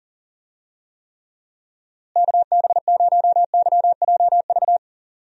Morse code, heard as a keyed tone, about 40 wpm, 700 Hz. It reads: KB0YJV
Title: USA Call Signs - 6 Characters - Repeated 2x - 40wpm